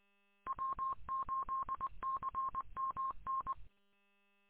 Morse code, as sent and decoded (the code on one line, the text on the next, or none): .-- ---.. -.-. -- -.
W8CMN